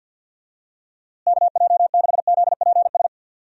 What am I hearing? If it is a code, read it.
Morse code, heard as a keyed tone, about 50 wpm, 700 Hz. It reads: KJ6ZPS
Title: USA Call Signs - 6 Characters - Repeated 2x - 50wpm